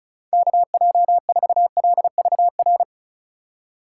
Morse code, heard as a keyed tone, about 36 wpm, 700 Hz. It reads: KJ4LVR
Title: USA Call Signs - 6 Characters - 6x Character Spacing - 35wpm